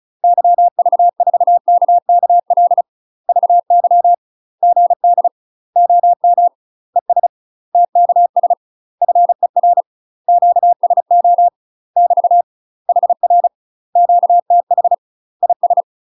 Morse code, heard as a keyed tone, about 36 wpm, 700 Hz. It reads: YV4KKL VY GD OM ES TKS FER QSO <BT> HR QTH IS
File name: UK MASTER QSOs TRAINING / 35WPM / UK MASTER QSOs 0007